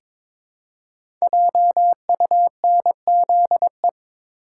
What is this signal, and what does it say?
Morse code, heard as a keyed tone, about 22 wpm, 700 Hz. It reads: JUNZE
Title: Code Groups - 5 Letters - 22wpm